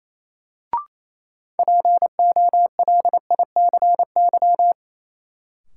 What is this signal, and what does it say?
Morse code, heard as a keyed tone, about 28 wpm, 700 Hz. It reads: POLICY
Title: Sentences from Top 1000 Words - Review Part 5 - New Word Drill - 28wpm